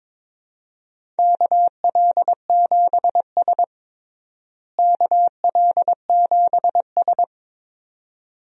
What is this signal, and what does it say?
Morse code, heard as a keyed tone, about 22 wpm, 700 Hz. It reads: KL7S KL7S
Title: DXCC-100 Call Signs - 4 Characters - Repeated 2x - 22wpm